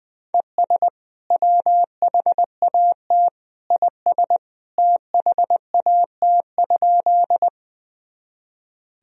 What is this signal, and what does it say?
Morse code, heard as a keyed tone, about 20 wpm, 700 Hz. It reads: ES WHAT IS THAT?
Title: Sentences from Top 100 Words and Abbreviations 20wpm